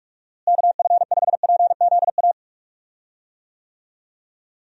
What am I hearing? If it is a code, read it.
Morse code, heard as a keyed tone, about 46 wpm, 700 Hz. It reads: KF5PZA
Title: USA Call Signs - 6 Characters - Repeated 3x - 45wpm